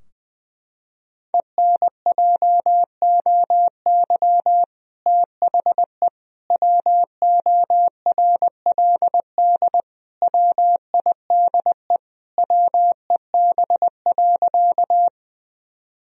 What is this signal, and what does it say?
Morse code, heard as a keyed tone, about 20 wpm, 700 Hz. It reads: ENJOY THE WORLD WIDE WEB.
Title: Sentences from Top 1000 Words - Review Part 4 - 20wpm